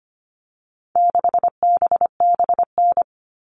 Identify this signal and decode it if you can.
Morse code, heard as a keyed tone, about 25 wpm, 700 Hz. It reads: 6BBN
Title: All Cut Numbers - 4-Digit 25wpm